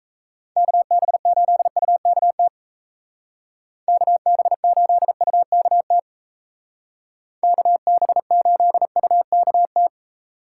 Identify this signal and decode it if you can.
Morse code, heard as a keyed tone, about 36 wpm, 700 Hz. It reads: KB8UKT KB8UKT KB8UKT